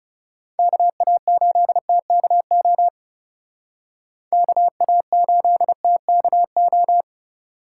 Morse code, heard as a keyed tone, about 34 wpm, 700 Hz. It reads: KA8TKO KA8TKO
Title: Speed Racing - USA 2x3 Call Signs 30wpm